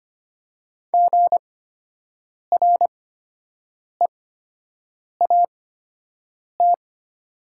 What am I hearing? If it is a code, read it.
Morse code, heard as a keyed tone, about 25 wpm, spaced as for 7 wpm, 700 Hz. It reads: GREAT
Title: 8x Character Spacing - Top 300 Words - 25wpm